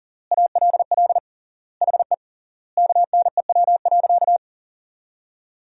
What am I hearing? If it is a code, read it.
Morse code, heard as a keyed tone, about 40 wpm, 700 Hz. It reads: ALL HE KNEW.
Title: 3x Word-Spacing - Sentences from Top 400 Words - 40wpm